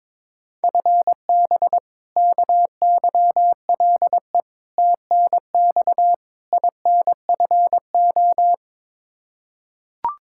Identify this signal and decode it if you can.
Morse code, heard as a keyed tone, about 22 wpm, 700 Hz. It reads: FB KYLE TNX INFO
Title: Practice QSO - Exchange 3 - More Info - 22wpm